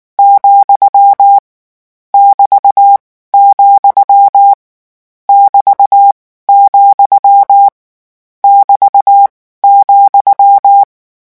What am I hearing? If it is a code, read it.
Morse code, heard as a keyed tone, about 19 wpm, 800 Hz. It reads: , <BT> , <BT> , <BT> ,
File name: morse mp3 / lecon14-vitesse 18